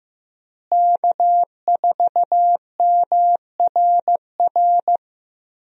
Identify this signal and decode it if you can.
Morse code, heard as a keyed tone, about 15 wpm, 700 Hz. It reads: K4MRR
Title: USA Call Signs - 5 Characters - Repeated 3x - 15wpm